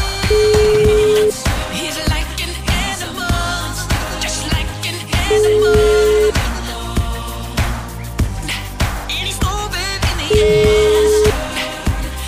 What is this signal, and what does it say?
Ringback tone (Europe, 425 Hz), with noise.